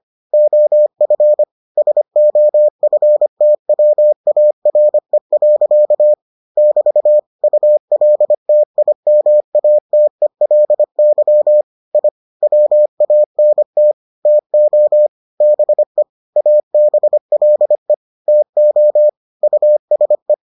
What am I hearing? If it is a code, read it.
Morse code, heard as a keyed tone, about 25 wpm, 600 Hz. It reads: OF SOFTWARE. <BT> ULTIMATELY I WANT TO BE ABLE TO USE